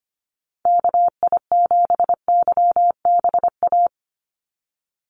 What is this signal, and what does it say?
Morse code, heard as a keyed tone, about 25 wpm, 700 Hz. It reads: KI7YBA
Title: USA Call Signs 25wpm